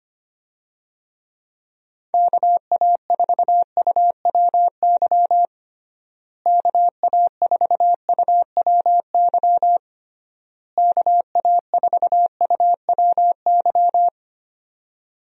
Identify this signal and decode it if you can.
Morse code, heard as a keyed tone, about 25 wpm, 700 Hz. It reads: KA4UWY KA4UWY KA4UWY
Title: USA Call Signs - 6 Characters - Repeated 3x - 25wpm